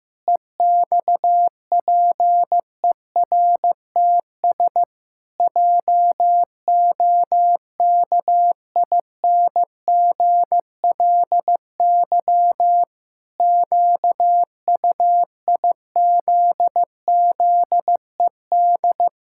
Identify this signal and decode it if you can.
Morse code, heard as a keyed tone, about 15 wpm, 700 Hz. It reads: EXPERTS JOKINGLY QUIZZED
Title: Pangram Challenge v2 - 15wpm